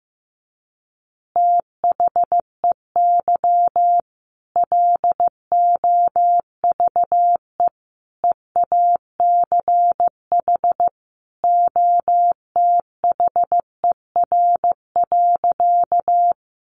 Morse code, heard as a keyed tone, about 15 wpm, 700 Hz. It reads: THEY LOVE EACH OTHER.
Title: Easy Phrases 15wpm